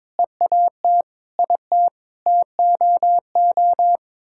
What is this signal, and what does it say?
Morse code, heard as a keyed tone, about 22 wpm, 700 Hz. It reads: EAT IT TOO
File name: Sets of 9 Words - 22wpm